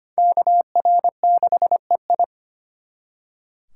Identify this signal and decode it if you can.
Morse code, heard as a keyed tone, about 32 wpm, 700 Hz. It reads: KR6EI